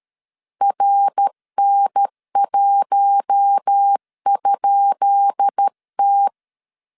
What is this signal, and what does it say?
Morse code, heard as a keyed tone, about 13 wpm, 800 Hz. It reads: RN1?T
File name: LEZIONI / 10 SIGNS / 10B INT 12wpm 2414